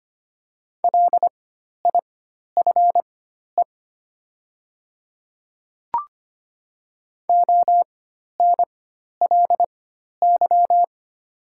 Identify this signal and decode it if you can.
Morse code, heard as a keyed tone, about 25 wpm, spaced as for 12 wpm, 700 Hz. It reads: LIFE ONLY
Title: 4x Character Spacing - Top 300 Words - 25wpm